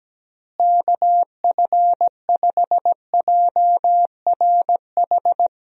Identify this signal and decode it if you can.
Morse code, heard as a keyed tone, about 17 wpm, 700 Hz. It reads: KF5JRH
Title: USA Call Signs - 6 Characters - Repeated 2x - 17wpm